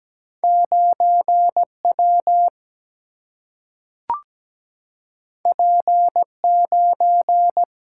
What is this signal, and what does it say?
Morse code, heard as a keyed tone, about 17 wpm, 700 Hz. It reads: 9 W P 9
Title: Single-Letter Single-Number 17wpm